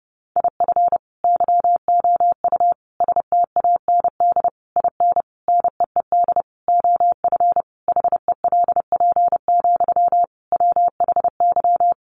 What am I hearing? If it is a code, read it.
Morse code, heard as a keyed tone, about 28 wpm, 700 Hz. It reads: IF YOU STAND IN NEED OF HELP, WHY
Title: Sentences from Top 1200 Words 30wpm